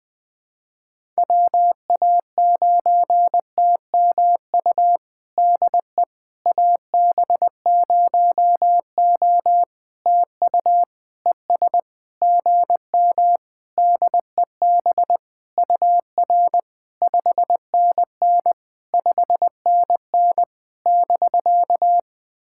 Morse code, heard as a keyed tone, about 20 wpm, 700 Hz. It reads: WA9TMU DE AB0O TU ES GM DEB UR 5NN 5NN <BK>
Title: POTA QSO Practice - 20wpm